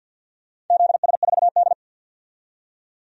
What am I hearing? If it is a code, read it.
Morse code, heard as a keyed tone, about 53 wpm, 700 Hz. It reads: ZS4D
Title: DXCC-100 Call Signs - 4 Characters - 50wpm